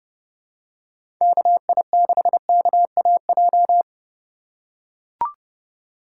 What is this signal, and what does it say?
Morse code, heard as a keyed tone, about 30 wpm, 690 Hz. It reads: KI6KAJ
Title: USA Call Signs - 3x Character Spacing - 30wpm